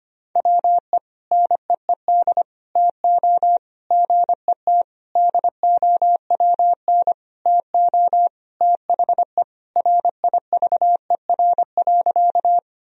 Morse code, heard as a keyed tone, about 25 wpm, 700 Hz. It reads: WE NEED TO GET DOWN TO THE RIVER.
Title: Sentences from Top 1000 Words - Review Part 1 - 25wpm